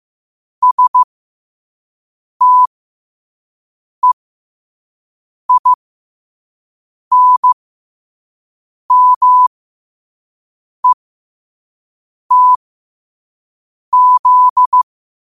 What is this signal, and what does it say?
Morse code, heard as a keyed tone, about 15 wpm, spaced as for 5 wpm, 1000 Hz. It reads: STEINMETZ
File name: qotdfiles / 5wpm